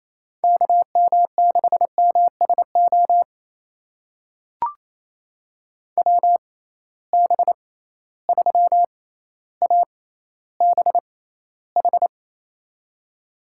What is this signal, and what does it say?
Morse code, heard as a keyed tone, about 28 wpm, spaced as for 14 wpm, 700 Hz. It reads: KM6MSO WB3ABH
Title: USA Call Signs - 6 Characters - 6x Character Spacing - 28wpm